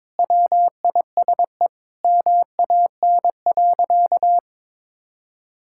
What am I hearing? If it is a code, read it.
Morse code, heard as a keyed tone, about 21 wpm, 700 Hz. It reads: WISE MAN.
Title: Sentences from Top 1600 Words 22wpm